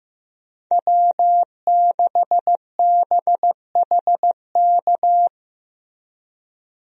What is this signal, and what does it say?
Morse code, heard as a keyed tone, about 15 wpm, 700 Hz. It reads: W6BHK
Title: USA Call Signs - 5 Characters - Repeated 2x - 15wpm